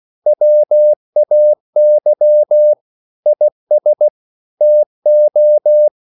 Morse code, heard as a keyed tone, about 16 wpm, 600 Hz. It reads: WAY IS TO
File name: CW / Practice / EnglishQuotations0216